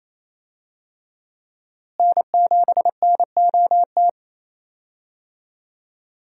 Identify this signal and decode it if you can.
Morse code, heard as a keyed tone, about 28 wpm, 700 Hz. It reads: N7NOT
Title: USA Call Signs - 5 Characters - Repeated 3x - 28wpm